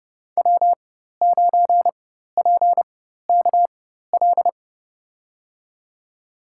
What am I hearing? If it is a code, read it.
Morse code, heard as a keyed tone, about 30 wpm, spaced as for 14 wpm, 700 Hz. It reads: W9PKL
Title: USA Call Signs - 5 Characters - 4x Character Spacing - 30wpm